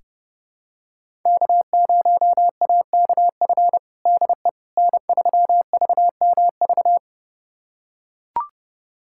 Morse code, heard as a keyed tone, about 30 wpm, 700 Hz. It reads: K0AKF DE N3VMV
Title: USA Callsign DE Callsign - 5 Characters - 30wpm